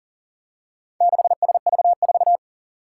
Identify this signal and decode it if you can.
Morse code, heard as a keyed tone, about 40 wpm, 700 Hz. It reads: 6SV4
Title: Four Challenging Letter-Number Combinations 40wpm